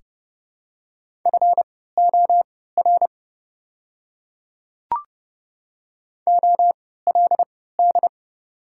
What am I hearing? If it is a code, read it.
Morse code, heard as a keyed tone, about 30 wpm, spaced as for 17 wpm, 700 Hz. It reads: FOR OLD
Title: 3x Character Spacing - Top 200 Words - 30wpm